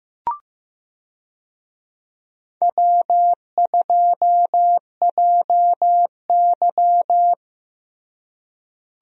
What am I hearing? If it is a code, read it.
Morse code, heard as a keyed tone, about 15 wpm, 700 Hz. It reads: W2JY
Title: USA Call Signs - 4 Characters - Repeated 3x - 15wpm